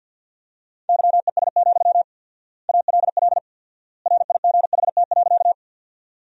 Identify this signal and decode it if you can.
Morse code, heard as a keyed tone, about 53 wpm, 700 Hz. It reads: YES, ALL RIGHT.
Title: 4x Word-Spacing - Sentences from Top 500 Words - 50wpm